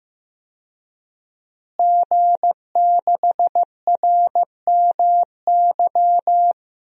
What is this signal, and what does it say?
Morse code, heard as a keyed tone, about 15 wpm, 700 Hz. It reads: G6RMY
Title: DXCC-100 Call Signs - 5 Characters - Repeated 2x - 15wpm